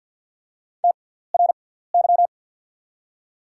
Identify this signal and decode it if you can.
Morse code, heard as a keyed tone, about 47 wpm, spaced as for 17 wpm, 690 Hz. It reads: TRY
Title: 6x Character Spacing - Top 200 Words - 50wpm